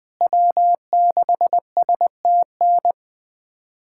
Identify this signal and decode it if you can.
Morse code, heard as a keyed tone, about 20 wpm, 700 Hz. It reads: W6STN
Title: USA Callsign DE Callsign 20wpm